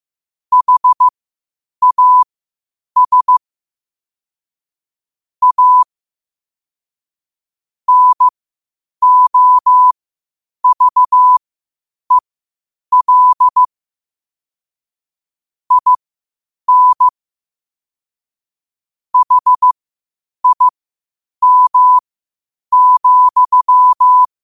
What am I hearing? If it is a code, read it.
Morse code, heard as a keyed tone, about 15 wpm, spaced as for 8 wpm, 1000 Hz. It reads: HAS A NOVEL IN HIM,
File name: qotdfiles / 8wpm